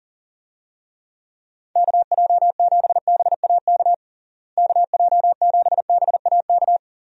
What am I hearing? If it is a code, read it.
Morse code, heard as a keyed tone, about 40 wpm, 700 Hz. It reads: KJ7BAK KJ7BAK
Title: USA Call Signs - 6 Characters - Repeated 2x - 40wpm